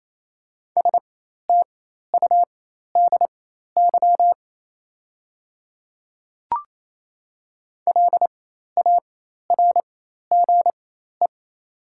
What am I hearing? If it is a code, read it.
Morse code, heard as a keyed tone, about 27 wpm, spaced as for 13 wpm, 690 Hz. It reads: STUDY LARGE
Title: 4x Character Spacing - Top 500 Words - 28wpm